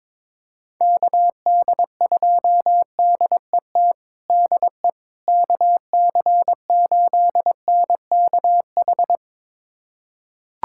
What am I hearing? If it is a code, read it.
Morse code, heard as a keyed tone, about 22 wpm, 700 Hz. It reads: KD2DET DE KC8NKH
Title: USA Callsign DE Callsign - 6 Characters - 22wpm